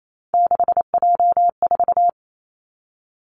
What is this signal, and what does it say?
Morse code, heard as a keyed tone, about 28 wpm, 700 Hz. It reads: 6J4
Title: Three Challenging Letter-Number Combinations 28wpm